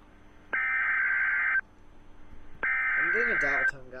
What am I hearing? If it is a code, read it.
Emergency Alert System (SAME) tones (header bursts), heard as harsh buzzing bursts, with speech. It is heard from 0.6 to 3.6 s.